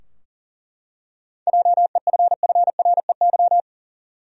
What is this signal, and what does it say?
Morse code, heard as a keyed tone, about 40 wpm, 700 Hz. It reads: JEFFREY
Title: USA Common Names - 40wpm